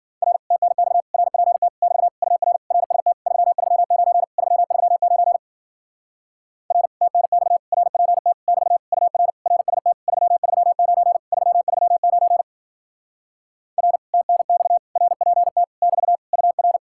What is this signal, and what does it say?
Morse code, heard as a keyed tone, about 52 wpm, 700 Hz. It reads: R TNX RPT <BT> UR RST 339 339 R TNX RPT <BT> UR RST 339 339 R TNX RPT <BT> UR